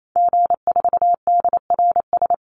Morse code, heard as a keyed tone, about 28 wpm, 700 Hz. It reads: G4DRS
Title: Call Signs - 28wpm